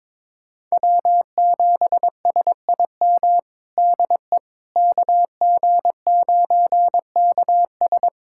Morse code, heard as a keyed tone, about 22 wpm, 700 Hz. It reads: W7SIM DE KG9KS
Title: USA Callsign DE Callsign - 5 Characters - 22wpm